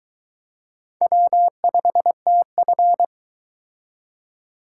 Morse code, heard as a keyed tone, about 22 wpm, 700 Hz. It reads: W5TF